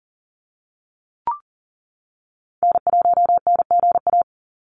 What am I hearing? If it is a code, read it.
Morse code, heard as a keyed tone, about 40 wpm, 700 Hz. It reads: N1NGA
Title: USA Call Signs 40wpm